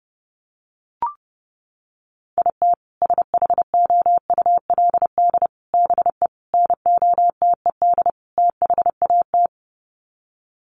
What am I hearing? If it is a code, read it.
Morse code, heard as a keyed tone, about 30 wpm, 700 Hz. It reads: IT SHOULD BE NOTED THAT
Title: Sets of 5 Words 30wpm